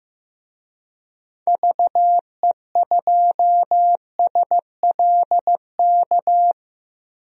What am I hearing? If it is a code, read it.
Morse code, heard as a keyed tone, about 15 wpm, 700 Hz. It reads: VE2SLK
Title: DXCC-100 Call Signs - 6 Characters - 15wpm